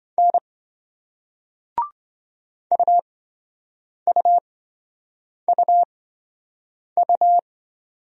Morse code, heard as a keyed tone, about 27 wpm, spaced as for 7 wpm, 700 Hz. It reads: N UUUU